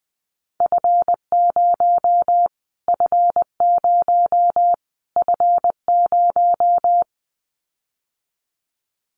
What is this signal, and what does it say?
Morse code, heard as a keyed tone, about 20 wpm, 700 Hz. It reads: F0 F0 F0
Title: Callsign Prefixes for Top 25 DXCC - Repeated 3x - 20wpm